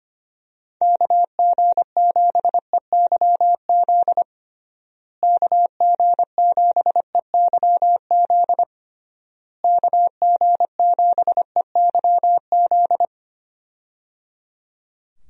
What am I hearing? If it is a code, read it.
Morse code, heard as a keyed tone, about 25 wpm, 700 Hz. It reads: KG7EYZ KG7EYZ KG7EYZ
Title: USA Call Signs - 6 Characters - Repeated 3x - 25wpm